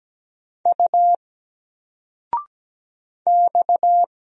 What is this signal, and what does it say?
Morse code, heard as a keyed tone, about 17 wpm, 700 Hz. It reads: U X
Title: Single Letter-Number 17wpm